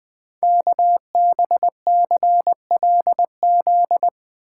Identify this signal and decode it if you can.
Morse code, heard as a keyed tone, about 20 wpm, 700 Hz. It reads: KBCLZ
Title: Code Groups - 5 Letters - 20wpm